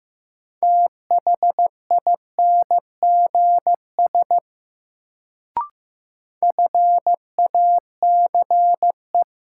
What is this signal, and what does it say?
Morse code, heard as a keyed tone, about 15 wpm, 700 Hz. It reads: THINGS FACE
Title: Sets of 3 Words - New Word Drill - 15wpm